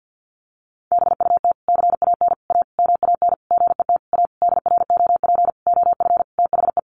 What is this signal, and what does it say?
Morse code, heard as a keyed tone, about 50 wpm, 700 Hz. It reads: BUT CAN A MAN GET A DROP OF THE
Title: Sentences from Top 1400 Words 50wpm